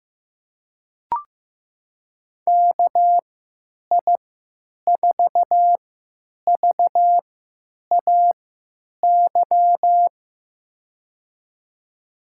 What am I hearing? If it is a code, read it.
Morse code, heard as a keyed tone, about 15 wpm, spaced as for 9 wpm, 700 Hz. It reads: KI4VAY
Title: USA Call Signs - 6 Characters - 3x Character Spacing - 15wpm